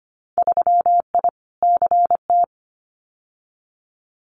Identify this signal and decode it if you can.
Morse code, heard as a keyed tone, about 25 wpm, 690 Hz. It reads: 3I CT
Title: Winter Field Day Exchanges 25wpm